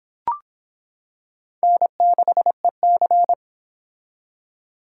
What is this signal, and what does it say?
Morse code, heard as a keyed tone, about 25 wpm, 700 Hz. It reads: N6EC